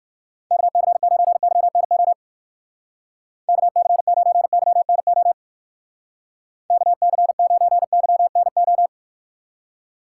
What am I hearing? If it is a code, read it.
Morse code, heard as a keyed tone, about 54 wpm, 700 Hz. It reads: KC9YNO KC9YNO KC9YNO
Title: Speed Racing - USA 2x3 Call Signs 45wpm